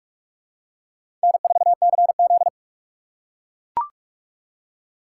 Morse code, heard as a keyed tone, about 42 wpm, 700 Hz. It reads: N4CZ
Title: USA Call Signs - 4 Characters - 3x Character Spacing - 45wpm